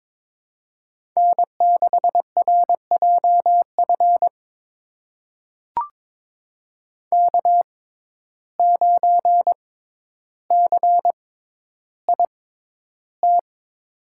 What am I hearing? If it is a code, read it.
Morse code, heard as a keyed tone, about 22 wpm, spaced as for 11 wpm, 700 Hz. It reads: N6RJF K9CIT
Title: USA Call Signs - 5 Characters - 6x Character Spacing - 22wpm